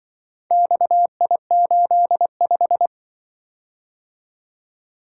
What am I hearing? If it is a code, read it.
Morse code, heard as a keyed tone, about 24 wpm, 700 Hz. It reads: XI85
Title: Speed-Racing - Four Letter-Number Combinations - 20wpm